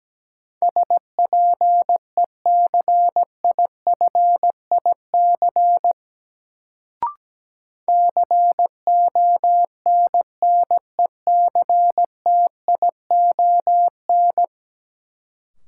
Morse code, heard as a keyed tone, about 17 wpm, 700 Hz. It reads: SPECIFIC CONNECTION
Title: Sentences from Top 1300 Words - New Word Drill Alt - 17wpm